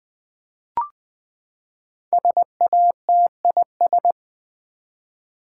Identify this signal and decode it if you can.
Morse code, heard as a keyed tone, about 20 wpm, 700 Hz. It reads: SATIS
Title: Build A Word - 5 Letters At A Time - 12-Letter Words - 20wpm